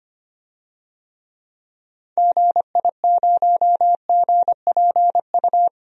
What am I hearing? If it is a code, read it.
Morse code, heard as a keyed tone, about 25 wpm, 700 Hz. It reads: GI0GPU
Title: DXCC-100 Call Signs - 6 Characters - 25wpm